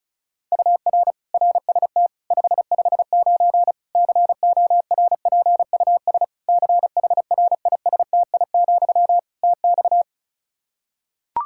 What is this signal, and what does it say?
Morse code, heard as a keyed tone, about 36 wpm, 700 Hz. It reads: UR RST 559 CORPUS CHRISTI, TX
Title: Practice QSO - Exchange 1 - Beginning - 35wpm